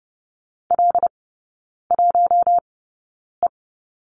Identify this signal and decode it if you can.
Morse code, heard as a keyed tone, about 30 wpm, spaced as for 9 wpm, 700 Hz. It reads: L1E
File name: Lesson 16e - Letter c - Three Characters 30wpm